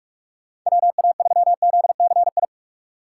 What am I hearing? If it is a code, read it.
Morse code, heard as a keyed tone, about 46 wpm, 700 Hz. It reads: WA3ZCI
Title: USA Call Signs - 6 Characters - 3x Character Spacing - 45wpm